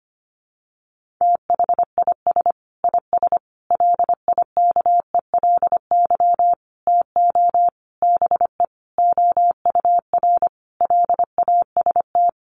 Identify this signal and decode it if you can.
Morse code, heard as a keyed tone, about 25 wpm, 700 Hz. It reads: THIS IS LIKELY TO BE OUR LAST
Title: Sentences from Top 1400 Words 25wpm